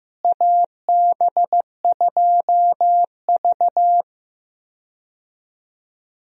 Morse code, heard as a keyed tone, about 15 wpm, 700 Hz. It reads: AB2V
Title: USA Call Signs - 4 Characters - Repeated 2x - 15wpm